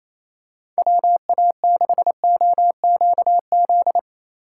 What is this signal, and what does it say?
Morse code, heard as a keyed tone, about 28 wpm, 700 Hz. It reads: WA6OQZ
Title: USA Call Signs - 6 Characters - 2x Character Spacing - 28wpm